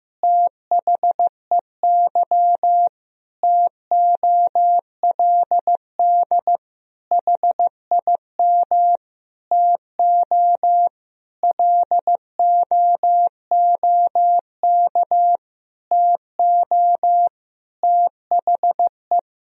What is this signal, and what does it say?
Morse code, heard as a keyed tone, about 16 wpm, 700 Hz. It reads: THEY TOLD HIM TO LOOK TO THE
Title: Sentences from Top 1000 Words - Review Part 3 - 15wpm